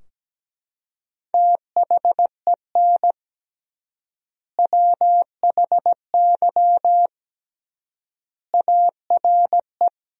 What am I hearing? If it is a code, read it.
Morse code, heard as a keyed tone, about 17 wpm, 700 Hz. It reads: THEN WHY ARE
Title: 3x Word-Spacing - Sentences from Top 200 Words - 17wpm